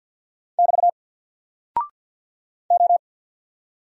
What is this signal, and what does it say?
Morse code, heard as a keyed tone, about 48 wpm, 700 Hz. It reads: <BT> O
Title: Single Character (Letter:Number:Punctuation:Prosign) 50wpm